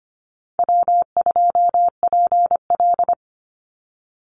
Morse code, heard as a keyed tone, about 25 wpm, 700 Hz. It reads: W2PL
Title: Easy Call Signs - 25wpm